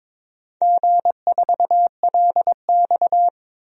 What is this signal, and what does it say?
Morse code, heard as a keyed tone, about 22 wpm, 700 Hz. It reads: G4LX
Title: DXCC-100 Call Signs - 4 Characters - Repeated 3x - 22wpm